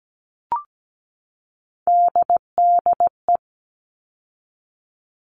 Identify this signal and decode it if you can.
Morse code, heard as a keyed tone, about 17 wpm, 700 Hz. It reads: DDE
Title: All Cut Numbers - 3-Digit 17wpm